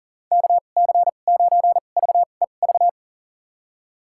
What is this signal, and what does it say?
Morse code, heard as a keyed tone, about 38 wpm, spaced as for 28 wpm, 700 Hz. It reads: KC9VEV
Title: USA Call Signs - 6 Characters - 2x Character Spacing - 40wpm